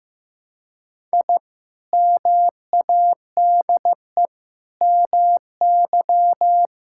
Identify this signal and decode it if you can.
Morse code, heard as a keyed tone, about 14 wpm, 700 Hz. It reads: I MADE MY
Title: Sentences from Top 1000 Words - Review Part 6 - 15wpm